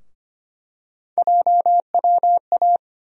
Morse code, heard as a keyed tone, about 25 wpm, 700 Hz. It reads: JWA
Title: USA Call Sign Postfixes - 3 Letters 25wpm